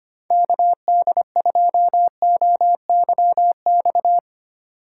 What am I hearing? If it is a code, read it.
Morse code, heard as a keyed tone, about 24 wpm, 700 Hz. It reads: KD2OYX
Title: Speed Racing - USA Call Signs 25wpm